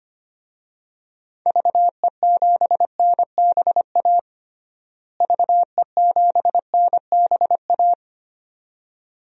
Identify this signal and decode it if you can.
Morse code, heard as a keyed tone, about 25 wpm, 700 Hz. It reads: VE7NBA VE7NBA
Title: DXCC-100 Call Signs - 6 Characters - Repeated 2x - 25wpm